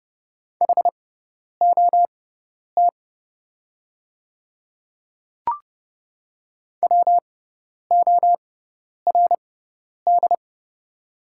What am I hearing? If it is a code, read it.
Morse code, heard as a keyed tone, about 30 wpm, spaced as for 10 wpm, 700 Hz. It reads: HOT WORD
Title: 6x Character Spacing - Top 100 Words - 30wpm